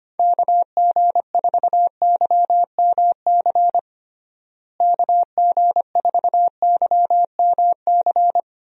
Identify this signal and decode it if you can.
Morse code, heard as a keyed tone, about 25 wpm, 700 Hz. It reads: KG4YMC KG4YMC
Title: USA Call Signs - 6 Characters - Repeated 2x - 25wpm